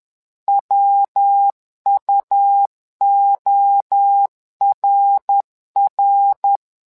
Morse code, heard as a keyed tone, about 10 wpm, 800 Hz. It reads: WUORR
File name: LEZIONI / 3 MWRO / 3 MWRO 10wpm 2148